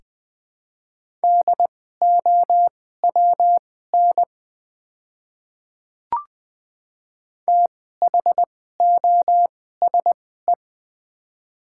Morse code, heard as a keyed tone, about 20 wpm, spaced as for 14 wpm, 700 Hz. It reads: DOWN THOSE
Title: 2x Character Spacing - Top 300 Words - 20wpm